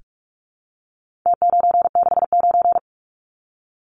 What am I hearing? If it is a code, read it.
Morse code, heard as a keyed tone, about 44 wpm, 700 Hz. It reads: T969
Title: Numbers - Contest Style - 4-Digit 45wpm